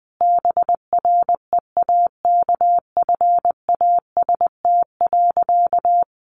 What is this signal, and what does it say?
Morse code, heard as a keyed tone, about 20 wpm, 700 Hz. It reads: BREAKFAST.
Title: Easy Phrases 20wpm